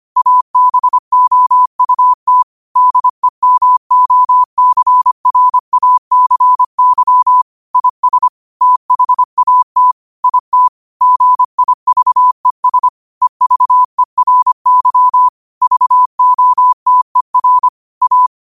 Morse code, heard as a keyed tone, about 25 wpm, 1000 Hz. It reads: ABOUT DEMOCRACY IS THAT IT GIVES EVERY VOTER A
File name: qotdfiles / 25wpm